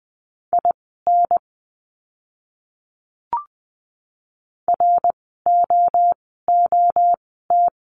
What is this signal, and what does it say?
Morse code, heard as a keyed tone, about 20 wpm, spaced as for 14 wpm, 700 Hz. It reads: IN ROOT
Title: 061 Lesson 11f - Letter h - Words 20wpm